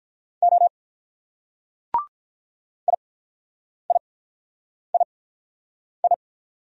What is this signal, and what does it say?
Morse code, heard as a keyed tone, about 48 wpm, 700 Hz. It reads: O I I I I